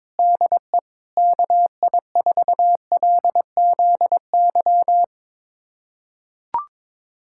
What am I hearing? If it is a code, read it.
Morse code, heard as a keyed tone, about 21 wpm, 700 Hz. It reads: DE KI4LZY
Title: USA Callsign DE Callsign - 6 Characters - 22wpm